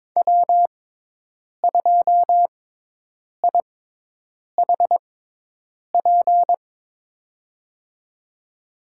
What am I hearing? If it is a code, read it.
Morse code, heard as a keyed tone, about 22 wpm, spaced as for 8 wpm, 700 Hz. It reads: W2IHP
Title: USA Call Signs - 5 Characters - 6x Character Spacing - 22wpm